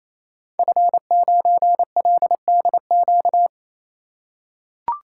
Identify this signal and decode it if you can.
Morse code, heard as a keyed tone, about 29 wpm, 700 Hz. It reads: F9LDQ
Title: Code Groups - 5 Letter:Numbers - 28wpm